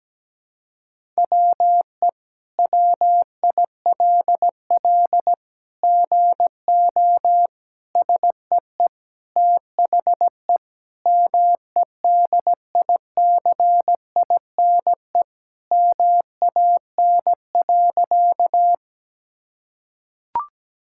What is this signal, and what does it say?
Morse code, heard as a keyed tone, about 17 wpm, 700 Hz. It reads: WE WILL GO SEE THE MEDICINE MAN.
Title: Sentences from Top 1000 Words - Review Part 6 - 17wpm